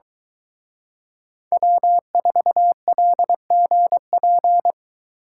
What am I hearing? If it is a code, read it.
Morse code, heard as a keyed tone, about 23 wpm, 700 Hz. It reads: W4LGP